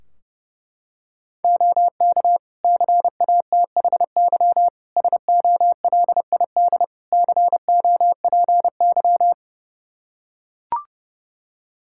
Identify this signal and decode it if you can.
Morse code, heard as a keyed tone, about 30 wpm, 700 Hz. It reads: OK CATHY SOLID COPY
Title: K1USN SST QSO Practice 30wpm